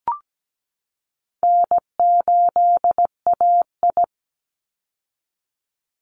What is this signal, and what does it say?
Morse code, heard as a keyed tone, about 17 wpm, 700 Hz. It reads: N8AI
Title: Call Signs Repeated 2x - 17wpm